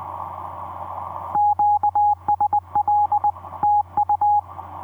Morse code, heard as a keyed tone, about 20 wpm, 830 Hz. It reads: QSL TU
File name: v5-dk1ce dm1tde etc 17m cw 28dec2010 0942